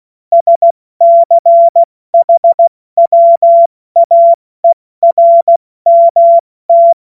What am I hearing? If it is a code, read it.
Morse code, heard as a keyed tone, about 16 wpm, 680 Hz. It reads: SCHWAERMT